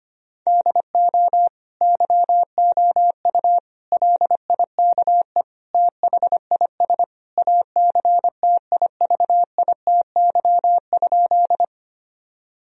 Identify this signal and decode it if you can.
Morse code, heard as a keyed tone, about 25 wpm, 700 Hz. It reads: DO YOU LIKE THIS ACTIVITY?
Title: Sentences from Top 1000 Words - Review Part 4 - 25wpm